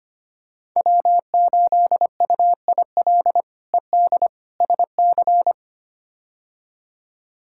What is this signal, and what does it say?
Morse code, heard as a keyed tone, about 25 wpm, 700 Hz. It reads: W8UIL ED SC
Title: K1USN SST QSO Practice - 25wpm